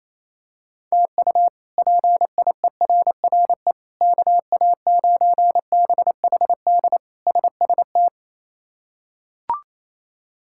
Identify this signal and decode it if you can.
Morse code, heard as a keyed tone, about 28 wpm, 700 Hz. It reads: TU PIERRE KA9BHD SST
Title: K1USN SST QSO Practice - 28wpm